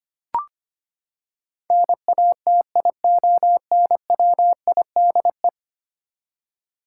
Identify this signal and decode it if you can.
Morse code, heard as a keyed tone, about 25 wpm, 700 Hz. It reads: NATIONWIDE
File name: Ten Letter Words - 25wpm